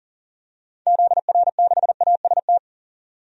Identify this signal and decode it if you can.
Morse code, heard as a keyed tone, about 40 wpm, 700 Hz. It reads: ZR6AST
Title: DXCC-100 Call Signs - 6 Characters - 40wpm